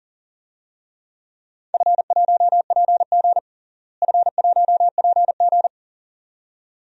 Morse code, heard as a keyed tone, about 40 wpm, 700 Hz. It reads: F1PG F1PG
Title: DXCC-100 Call Signs - 4 Characters - Repeated 2x - 40wpm